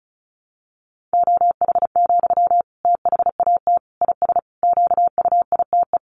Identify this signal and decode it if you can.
Morse code, heard as a keyed tone, about 34 wpm, 700 Hz. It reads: OH, THAT IS QUITE
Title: Sentences from Top 700 Words 35wpm